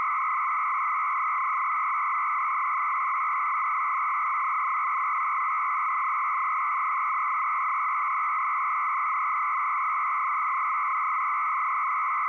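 300-baud modem data.
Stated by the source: 300-baud modem tones used as music.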